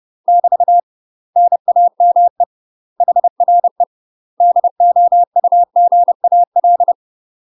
Morse code, heard as a keyed tone, about 30 wpm, 700 Hz. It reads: <BT> NAME HRE DOUGAL
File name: UK MASTER QSOs TRAINING / 30WPM / UK MASTER QSOs 0018